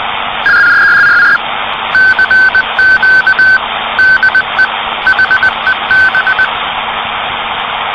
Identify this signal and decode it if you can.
Morse code, heard as a keyed tone, about 20 wpm, 1550 Hz, with noise. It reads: CQ DE HEB
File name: CW PACTOR CQ de HEB